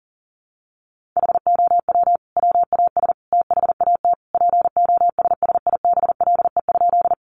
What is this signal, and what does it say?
Morse code, heard as a keyed tone, about 40 wpm, 700 Hz. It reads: HOW WAS THAT POSSIBLE?
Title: Sentences from Top 500 Words 40wpm